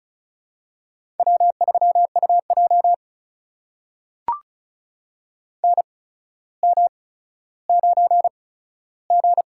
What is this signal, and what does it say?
Morse code, heard as a keyed tone, about 36 wpm, spaced as for 15 wpm, 700 Hz. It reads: W3UJ NM9G
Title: USA Call Signs - 4 Characters - 8x Character Spacing - 35wpm